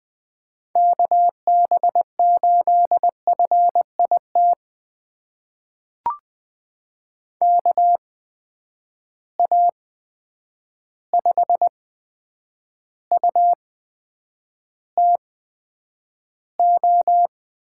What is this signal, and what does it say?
Morse code, heard as a keyed tone, about 20 wpm, spaced as for 9 wpm, 700 Hz. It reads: KB8FIT KA5UTO
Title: USA Call Signs - 8x Character Spacing - 20wpm